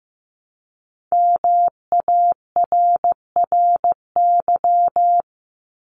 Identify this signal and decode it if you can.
Morse code, heard as a keyed tone, about 15 wpm, 700 Hz. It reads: MARRY
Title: Multi-letter Phonemes Repeated 3x 15wpm